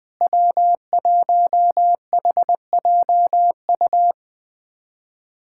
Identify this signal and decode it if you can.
Morse code, heard as a keyed tone, about 20 wpm, 700 Hz. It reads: W1HJU
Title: USA Callsign DE Callsign - 5 Characters - 20wpm